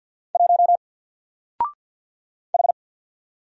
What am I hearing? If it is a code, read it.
Morse code, heard as a keyed tone, about 48 wpm, 700 Hz. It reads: 1 H
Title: Single Character (Letter:Number:Punctuation:Prosign) Alt 50wpm